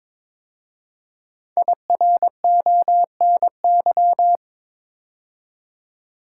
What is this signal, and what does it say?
Morse code, heard as a keyed tone, about 22 wpm, 700 Hz. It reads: IRONY
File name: 5 Letter Words - First 3 Letters Given - 22wpm